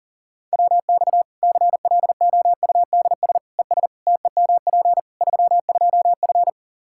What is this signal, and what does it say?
Morse code, heard as a keyed tone, about 40 wpm, 700 Hz. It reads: WX CLOUDS ES TEMP 32F
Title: Practice QSO - Exchange 2 - Info - 40wpm